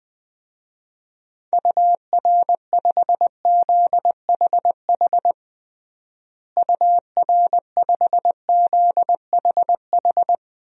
Morse code, heard as a keyed tone, about 21 wpm, 700 Hz. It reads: UR5ZHH UR5ZHH
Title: DXCC-100 Call Signs - 6 Characters - Repeated 3x - 20wpm